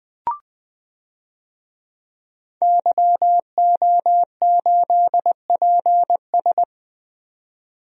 Morse code, heard as a keyed tone, about 21 wpm, 700 Hz. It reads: YO8PS
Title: DXCC-100 Call Signs - 5 Characters - 20wpm